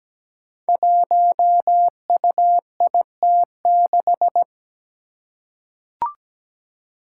Morse code, heard as a keyed tone, about 17 wpm, 700 Hz. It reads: 1UIT6
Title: Five Challenging Letter-Number Combinations 17wpm